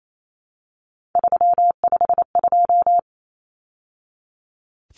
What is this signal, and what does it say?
Morse code, heard as a keyed tone, about 28 wpm, 700 Hz. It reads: 352
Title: Numbers - 3-Digit 28wpm